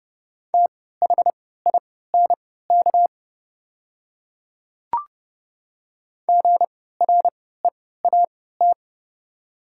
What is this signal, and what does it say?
Morse code, heard as a keyed tone, about 30 wpm, spaced as for 17 wpm, 700 Hz. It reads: THINK GREAT
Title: 3x Character Spacing - Top 200 Words - 30wpm